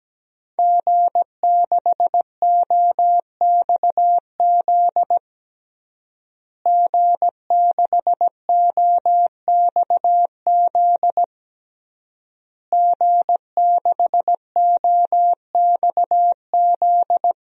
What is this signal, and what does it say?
Morse code, heard as a keyed tone, about 17 wpm, 700 Hz. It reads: G6OXZ G6OXZ G6OXZ
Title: DXCC-100 Call Signs - 5 Characters - Repeated 3x - 17wpm